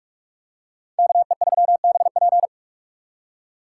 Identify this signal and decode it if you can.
Morse code, heard as a keyed tone, about 45 wpm, 700 Hz. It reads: KE3BP